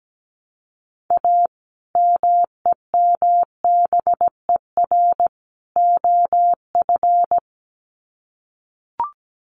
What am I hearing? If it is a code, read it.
Morse code, heard as a keyed tone, about 17 wpm, 700 Hz. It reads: A MEMBER OF
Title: Sets of 3 Words 17wpm